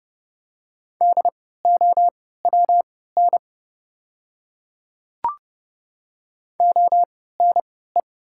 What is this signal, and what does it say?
Morse code, heard as a keyed tone, about 30 wpm, spaced as for 17 wpm, 700 Hz. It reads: DOWN ONE
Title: 3x Character Spacing - Top 100 Words - 30wpm